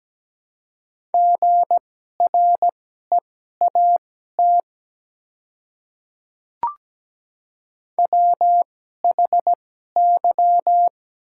Morse code, heard as a keyed tone, about 17 wpm, spaced as for 12 wpm, 700 Hz. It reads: GREAT WHY
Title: 2x Character Spacing - Top 200 Words - 17wpm